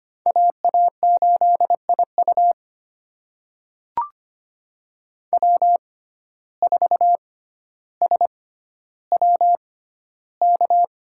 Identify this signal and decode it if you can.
Morse code, heard as a keyed tone, about 25 wpm, spaced as for 13 wpm, 700 Hz. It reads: AA8IU W4SWK
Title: USA Call Signs - 5 Characters - 6x Character Spacing - 25wpm